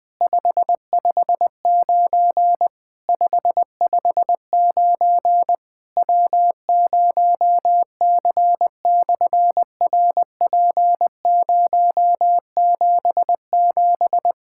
Morse code, heard as a keyed tone, about 20 wpm, 700 Hz. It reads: 559 559 W0C/RP077
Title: SOTA QSO Practice 20wpm